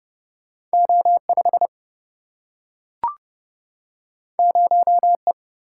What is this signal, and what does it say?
Morse code, heard as a keyed tone, about 31 wpm, 700 Hz. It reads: O5 0E
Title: Single-Letter Single-Number 30wpm